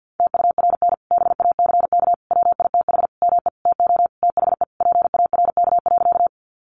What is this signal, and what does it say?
Morse code, heard as a keyed tone, about 50 wpm, 700 Hz. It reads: TURN BACK WITH ME TO THE PARK.
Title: Sentences from Top 1300 Words 50wpm